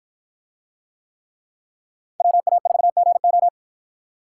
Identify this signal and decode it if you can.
Morse code, heard as a keyed tone, about 52 wpm, 700 Hz. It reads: WA4GO